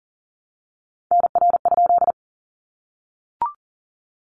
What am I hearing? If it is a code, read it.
Morse code, heard as a keyed tone, about 40 wpm, 700 Hz. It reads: NR?
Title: Common QSO Phrases 40wpm